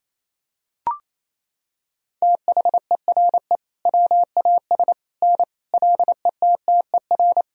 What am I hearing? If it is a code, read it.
Morse code, heard as a keyed tone, about 28 wpm, 700 Hz. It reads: THERE WAS N LETTER
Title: Sentences from Top 300 Words and Abbreviations 28wpm